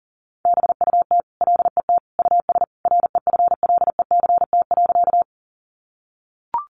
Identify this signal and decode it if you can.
Morse code, heard as a keyed tone, about 40 wpm, 700 Hz. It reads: BUT LET US REFLECT.
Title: Sentences from Top 1500 Words 40wpm